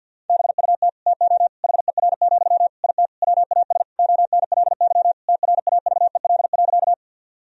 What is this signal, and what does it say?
Morse code, heard as a keyed tone, about 50 wpm, 700 Hz. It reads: BUT TO HER, IT WAS ONLY TRAVEL.
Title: Sentences from Top 1000 Words - Review Part 2 - 50wpm